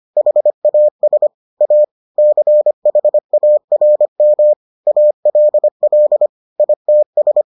Morse code, heard as a keyed tone, about 25 wpm, 600 Hz. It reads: HAS A CHARM ALL ITS